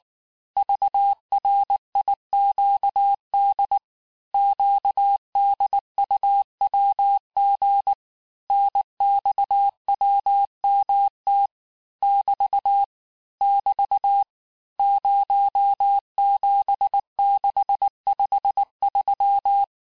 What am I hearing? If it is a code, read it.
Morse code, heard as a keyed tone, about 19 wpm, 800 Hz. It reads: VRIQD QDUWG NXWMT <BT> <BT> 07653
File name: cours-du-lundi files / cours-1583--texte-chiffre-18-mpm